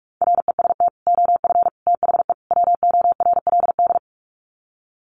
Morse code, heard as a keyed tone, about 44 wpm, 700 Hz. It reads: REST OF THE WORLD
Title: Sets of 4 Words 45wpm